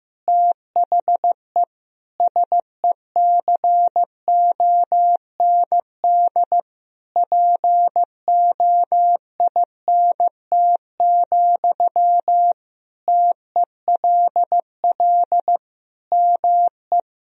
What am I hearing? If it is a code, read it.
Morse code, heard as a keyed tone, about 15 wpm, 700 Hz. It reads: THE SECOND POINT, TELL ME
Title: Sentences from Top 1000 Words - Review Part 1 - 15wpm